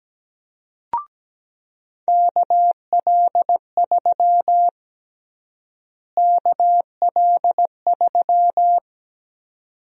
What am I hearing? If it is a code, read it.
Morse code, heard as a keyed tone, about 17 wpm, 700 Hz. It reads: KL3 KL3
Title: USA Call Sign Prefixes - Repeated 2x - 17wpm